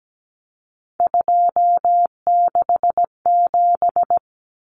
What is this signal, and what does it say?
Morse code, heard as a keyed tone, about 17 wpm, 700 Hz. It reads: 267
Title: Common Cut Numbers - 3-Digit 17wpm